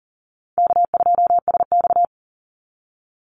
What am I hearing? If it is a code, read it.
Morse code, heard as a keyed tone, about 40 wpm, 700 Hz. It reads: K2SX
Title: Call Signs Repeated 3x - 40wpm